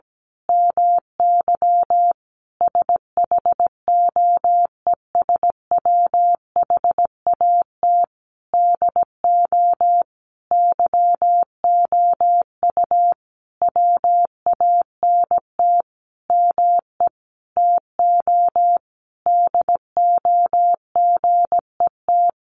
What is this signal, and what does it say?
Morse code, heard as a keyed tone, about 17 wpm, 700 Hz. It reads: MY SHOES WHAT DO YOU WANT ME TO DO GET
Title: A Sound Of Thunder 17wpm